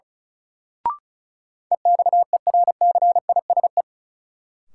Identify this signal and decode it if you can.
Morse code, heard as a keyed tone, about 36 wpm, 700 Hz. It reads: EXERCISE
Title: Sentences from Top 1200 Words - New Word Drill Alt - 35wpm